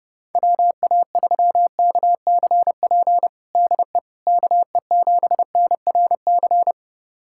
Morse code, heard as a keyed tone, about 30 wpm, 700 Hz. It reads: WA3KCP DE KE7NRC
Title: USA Callsign DE Callsign - 6 Characters - 30wpm